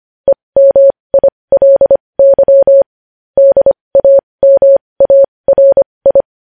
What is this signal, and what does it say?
Morse code, heard as a keyed tone, about 25 wpm, spaced as for 20 wpm, 550 Hz. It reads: EMILY DAMARI
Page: news-headlines-in-morse-code-at-20-wpm